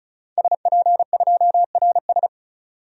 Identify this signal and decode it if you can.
Morse code, heard as a keyed tone, about 34 wpm, 700 Hz. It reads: SP2RS
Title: DXCC-100 Call Signs - 5 Characters - 35wpm